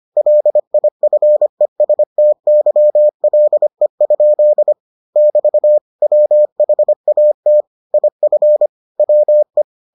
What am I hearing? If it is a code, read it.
Morse code, heard as a keyed tone, about 25 wpm, 600 Hz. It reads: LIFESTYLE? <BT> WHAT IF WE